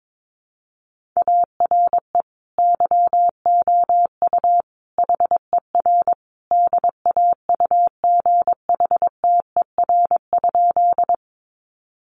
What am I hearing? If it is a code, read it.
Morse code, heard as a keyed tone, about 22 wpm, 700 Hz. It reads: ARE YOU HER DAUGHTER?
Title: Sentences from Top 1500 Words 22wpm